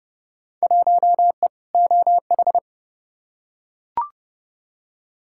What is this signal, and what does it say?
Morse code, heard as a keyed tone, about 30 wpm, 700 Hz. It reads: 1E OH
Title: Field Day Exchanges - 30wpm